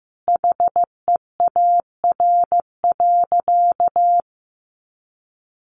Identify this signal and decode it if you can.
Morse code, heard as a keyed tone, about 15 wpm, 700 Hz. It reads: HEAR.
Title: A Sound Of Thunder 15wpm